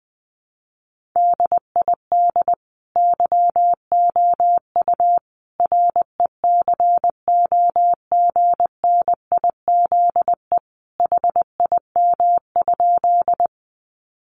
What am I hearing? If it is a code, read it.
Morse code, heard as a keyed tone, about 20 wpm, 700 Hz. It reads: DID YOU RECOGNIZE HIM?
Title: Sentences from Top 1400 Words 20wpm